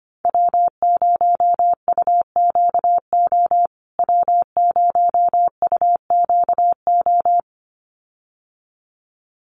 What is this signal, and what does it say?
Morse code, heard as a keyed tone, about 25 wpm, 700 Hz. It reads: W0UQO W0UQO
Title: USA Call Signs Repeated 2x 25wpm